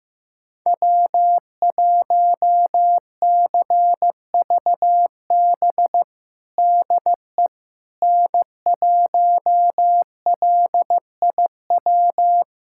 Morse code, heard as a keyed tone, about 16 wpm, 700 Hz. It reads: W1CVB DE N1LIW
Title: USA Callsign DE Callsign - 5 Characters - 15wpm